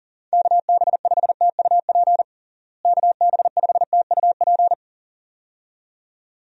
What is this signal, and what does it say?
Morse code, heard as a keyed tone, about 40 wpm, 700 Hz. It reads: KB5TUP KB5TUP
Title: USA Call Signs - 6 Characters - Repeated 2x - 40wpm